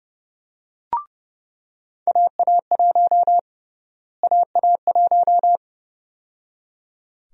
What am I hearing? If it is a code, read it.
Morse code, heard as a keyed tone, about 32 wpm, 700 Hz. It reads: AA1 AA1
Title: USA Call Sign Prefixes - Repeated 2x - 30wpm